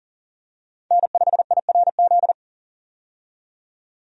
Morse code, heard as a keyed tone, about 40 wpm, 700 Hz. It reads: N5IRZ
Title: USA Call Signs - 5 Characters - 40wpm